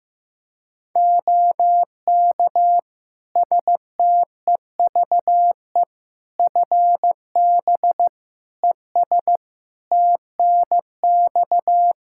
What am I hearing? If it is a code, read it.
Morse code, heard as a keyed tone, about 15 wpm, 700 Hz. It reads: OK STEVE FB ES TNX
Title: Practice QSO - Exchange 2 - Info - 15wpm